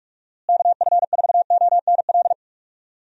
Morse code, heard as a keyed tone, about 41 wpm, 700 Hz. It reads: KF4ONL
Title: USA Call Signs - 6 Characters - Repeated 2x - 45wpm